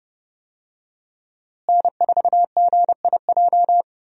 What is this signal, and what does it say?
Morse code, heard as a keyed tone, about 30 wpm, 700 Hz. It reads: N4GIJ
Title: USA Call Signs - 5 Characters - Repeated 2x - 30wpm